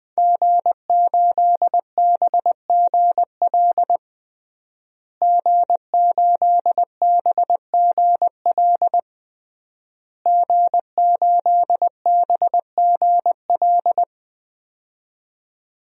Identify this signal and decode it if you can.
Morse code, heard as a keyed tone, about 20 wpm, 700 Hz. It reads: G8BGL G8BGL G8BGL
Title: DXCC-100 Call Signs - 5 Characters - Repeated 3x - 20wpm